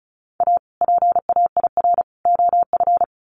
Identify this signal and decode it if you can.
Morse code, heard as a keyed tone, about 34 wpm, 700 Hz. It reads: A PAIR OF
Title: Sets of 3 Words 35wpm